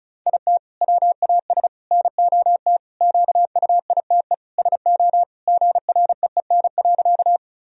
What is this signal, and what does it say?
Morse code, heard as a keyed tone, about 36 wpm, 700 Hz. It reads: IT WAS NOT QUITE SO GREEN.
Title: Sentences from Top 1000 Words - Review Part 2 - 35wpm